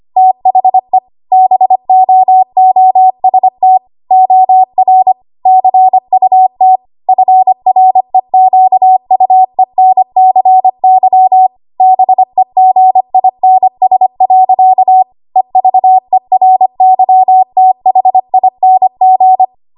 Morse code, heard as a keyed tone, about 25 wpm, 750 Hz. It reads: THE BOOST OR CUT FREQUENCY BEGINS. EVERYTHING
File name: Morse / 25WPM